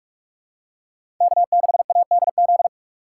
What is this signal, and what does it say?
Morse code, heard as a keyed tone, about 43 wpm, 700 Hz. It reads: K6ADZ
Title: USA Call Signs - 5 Characters - 6x Character Spacing - 45wpm